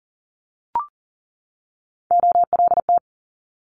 Morse code, heard as a keyed tone, about 39 wpm, 700 Hz. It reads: OLT
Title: Multi-letter Phonemes 40wpm